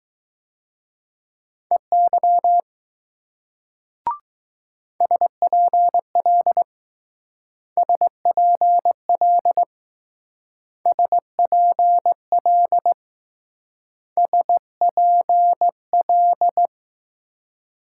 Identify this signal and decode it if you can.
Morse code, heard as a keyed tone, about 20 wpm, 700 Hz. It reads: EY SPL SPL SPL SPL